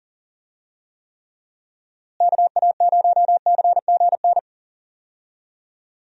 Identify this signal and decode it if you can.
Morse code, heard as a keyed tone, about 40 wpm, 700 Hz. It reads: KA0CGN
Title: USA Call Signs - 6 Characters - Repeated 3x - 40wpm